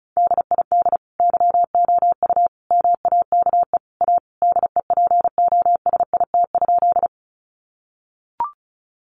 Morse code, heard as a keyed tone, about 34 wpm, 700 Hz. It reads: DID YOU MAKE A DEPOSIT?
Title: Sentences from Top 1300 Words 35wpm